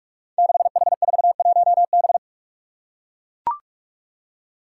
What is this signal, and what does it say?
Morse code, heard as a keyed tone, about 44 wpm, 700 Hz. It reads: 6H41B
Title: Five Challenging Letter-Number Combinations 45wpm